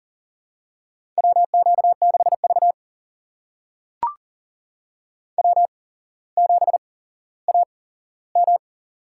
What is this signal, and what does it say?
Morse code, heard as a keyed tone, about 40 wpm, spaced as for 17 wpm, 700 Hz. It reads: WQ6V W7AM
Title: USA Call Signs - 4 Characters - 8x Character Spacing - 40wpm